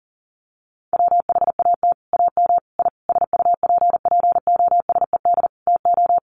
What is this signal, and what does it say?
Morse code, heard as a keyed tone, about 39 wpm, 700 Hz. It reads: WHAT AM I SUPPOSED TO
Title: Sets of 5 Words Encore2 40wpm